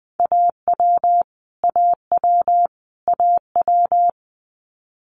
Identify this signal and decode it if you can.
Morse code, heard as a keyed tone, about 19 wpm, 700 Hz. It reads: AW AW AW
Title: Multi-letter Phonemes Repeated 3x 20wpm